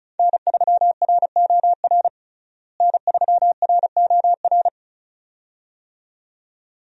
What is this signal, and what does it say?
Morse code, heard as a keyed tone, about 34 wpm, 700 Hz. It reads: N3ROR N3ROR
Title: USA Call Signs - 5 Characters - Repeated 2x - 35wpm